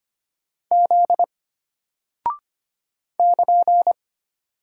Morse code, heard as a keyed tone, about 25 wpm, 700 Hz. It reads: Z <KN>
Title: Single Character (Letter:Number:Punctuation:Prosign) Alt 25wpm